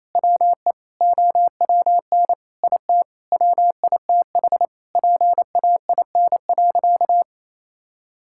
Morse code, heard as a keyed tone, about 28 wpm, 700 Hz. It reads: WE OWN IT WITH PAIN.
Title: Sentences from Top 1600 Words 28wpm